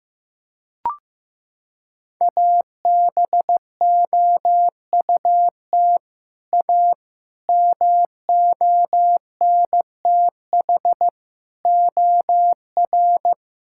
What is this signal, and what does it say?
Morse code, heard as a keyed tone, about 15 wpm, 700 Hz. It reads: ABOUT A MONTH OR
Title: Sentences from Top 1000 Words - Review Part 5 - 15wpm